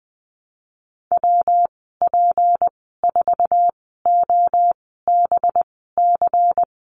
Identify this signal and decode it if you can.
Morse code, heard as a keyed tone, about 20 wpm, spaced as for 15 wpm, 700 Hz. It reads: WP4OBC
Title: 231 Lesson 37g - Letter x - Call Signs 20wpm